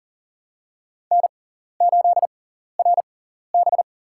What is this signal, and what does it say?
Morse code, heard as a keyed tone, about 37 wpm, spaced as for 13 wpm, 700 Hz. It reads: N8RB
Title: USA Call Signs - 6x Character Spacing - 40wpm